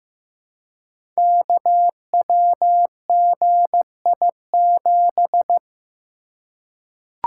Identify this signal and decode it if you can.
Morse code, heard as a keyed tone, about 15 wpm, 700 Hz. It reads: KWGI7
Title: Code Groups - 5 Letter:Numbers - 15wpm